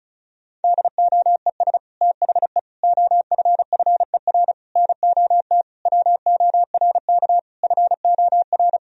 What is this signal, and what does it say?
Morse code, heard as a keyed tone, about 34 wpm, 700 Hz. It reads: DOES THE OFFER NOT WORK FOR
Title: Sentences from Top 1000 Words - Review Part 4 - 35wpm